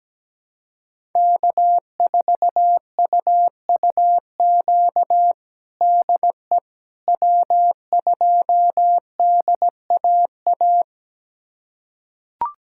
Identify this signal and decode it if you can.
Morse code, heard as a keyed tone, about 17 wpm, 700 Hz. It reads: K4UUQ DE W2DAA
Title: USA Callsign DE Callsign - 5 Characters - 17wpm